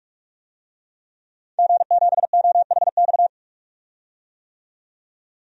Morse code, heard as a keyed tone, about 47 wpm, 700 Hz. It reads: G7OHX
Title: DXCC-100 Call Signs - 5 Characters - 45wpm